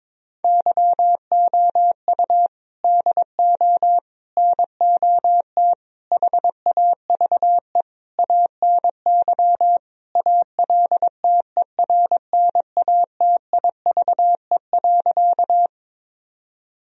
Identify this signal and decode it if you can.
Morse code, heard as a keyed tone, about 22 wpm, 700 Hz. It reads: YOU DO NOT HAVE ANY ALTERNATIVE.
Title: Sentences from Top 1000 Words - Review Part 4 - 22wpm